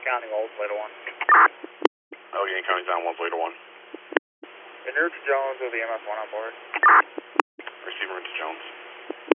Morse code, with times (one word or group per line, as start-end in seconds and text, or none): none